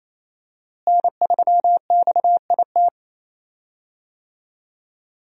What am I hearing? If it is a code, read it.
Morse code, heard as a keyed tone, about 28 wpm, 700 Hz. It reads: N3XIT
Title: USA Call Signs - 5 Characters - Repeated 3x - 28wpm